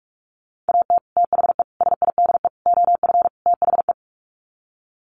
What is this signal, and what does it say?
Morse code, heard as a keyed tone, about 44 wpm, 700 Hz. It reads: AT THE SIDE OF THE
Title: Sets of 5 Words Encore2 45wpm